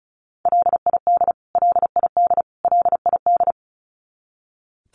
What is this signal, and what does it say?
Morse code, heard as a keyed tone, about 34 wpm, 700 Hz. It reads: LID LID LID
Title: Abbreviations Repeated 3x - 35wpm